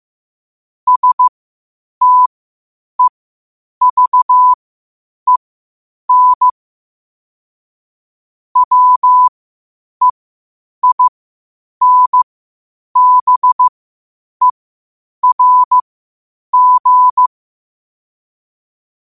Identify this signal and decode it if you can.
Morse code, heard as a keyed tone, about 15 wpm, spaced as for 8 wpm, 1000 Hz. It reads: STEVEN WEINBERG
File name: qotdfiles / 8wpm